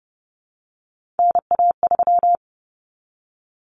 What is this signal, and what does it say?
Morse code, heard as a keyed tone, about 30 wpm, 700 Hz. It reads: NA3
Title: Common Cut Numbers - 3-Digit 30wpm